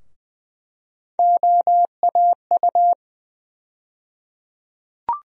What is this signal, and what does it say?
Morse code, heard as a keyed tone, about 20 wpm, 700 Hz. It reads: OAU
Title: USA Call Sign Postfixes - 3 Letters 20wpm